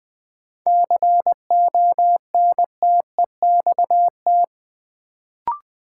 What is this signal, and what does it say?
Morse code, heard as a keyed tone, about 20 wpm, 700 Hz. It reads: CONTEXT
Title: Sets of 4 Words - New Word Drill - 20wpm